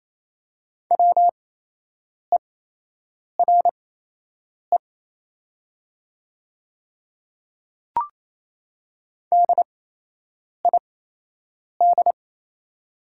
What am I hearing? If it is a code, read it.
Morse code, heard as a keyed tone, about 28 wpm, spaced as for 8 wpm, 700 Hz. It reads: WERE DID
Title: 8x Character Spacing - Top 500 Words - 28wpm